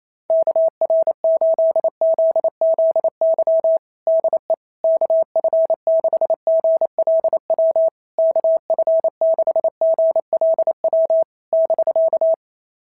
Morse code, heard as a keyed tone, about 28 wpm, 660 Hz. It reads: KR8ZZY DE KF6GLW KF6GLW <BK>
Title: SOTA QSO Practice 28wpm